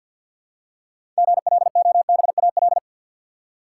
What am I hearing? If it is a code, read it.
Morse code, heard as a keyed tone, about 52 wpm, 700 Hz. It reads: GLOBAL
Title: Six Letter Words - 50wpm